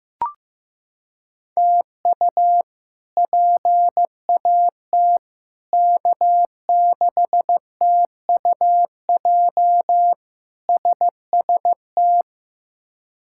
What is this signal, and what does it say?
Morse code, heard as a keyed tone, about 16 wpm, 700 Hz. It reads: TU PAT K6TUJ SST
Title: K1USN SST QSO Practice - 15wpm